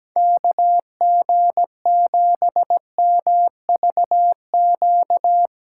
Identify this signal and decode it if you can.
Morse code, heard as a keyed tone, about 17 wpm, 700 Hz. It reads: KG7MVQ
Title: USA Callsign DE Callsign 17wpm